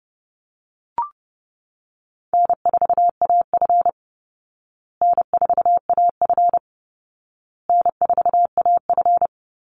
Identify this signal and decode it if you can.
Morse code, heard as a keyed tone, about 30 wpm, 700 Hz. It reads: N4AF N4AF N4AF
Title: Call Signs Repeated 2x - 30wpm